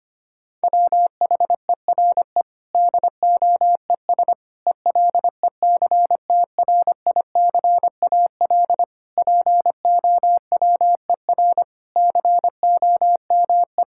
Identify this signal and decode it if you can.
Morse code, heard as a keyed tone, about 25 wpm, 700 Hz. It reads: WHERE DOES ELECTRICAL POWER COME
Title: Sentences from Top 1000 Words - Review Part 6 - 25wpm